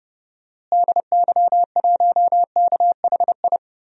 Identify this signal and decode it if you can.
Morse code, heard as a keyed tone, about 30 wpm, 700 Hz. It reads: DY1KHI
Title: DXCC-100 Call Signs - 6 Characters - Repeated 2x - 30wpm